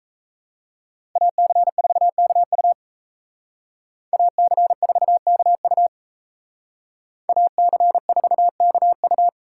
Morse code, heard as a keyed tone, about 35 wpm, 700 Hz. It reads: AC4KU AC4KU AC4KU